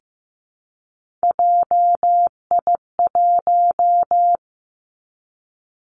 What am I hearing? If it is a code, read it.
Morse code, heard as a keyed tone, about 15 wpm, 700 Hz. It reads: JI1
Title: Three Challenging Letter-Number Combinations 15wpm